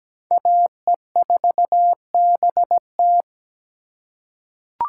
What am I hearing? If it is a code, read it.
Morse code, heard as a keyed tone, about 17 wpm, 700 Hz. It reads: AE4BT
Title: USA Call Signs - 5 Characters - 2x Character Spacing - 17wpm